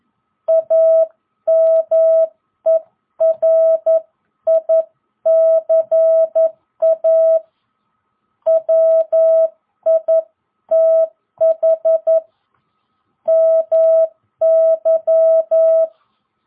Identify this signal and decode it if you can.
Morse code, heard as a keyed tone, about 11 wpm, 660 Hz. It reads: AMERICA WITH MY